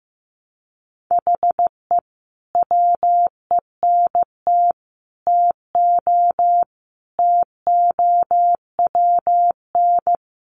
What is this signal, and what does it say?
Morse code, heard as a keyed tone, about 15 wpm, 700 Hz. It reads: HE WENT TO TOWN
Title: Sentences from Top 1300 Words 15wpm